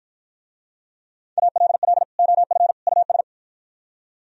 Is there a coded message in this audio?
Yes: Morse code from 1.4 to 3.2 s, heard as a keyed tone, about 54 wpm, 700 Hz. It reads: ALL OF US